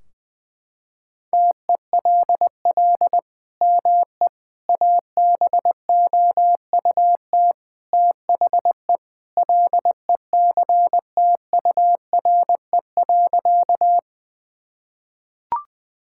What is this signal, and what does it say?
Morse code, heard as a keyed tone, about 20 wpm, 700 Hz. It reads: TELL ME ABOUT THE LECTURE.
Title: Sentences from Top 1600 Words 20wpm